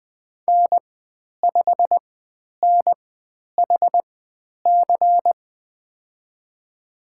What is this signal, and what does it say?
Morse code, heard as a keyed tone, about 20 wpm, spaced as for 10 wpm, 700 Hz. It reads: N5NHC
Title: 109 Lesson 18g - Letter m - Call Signs 20wpm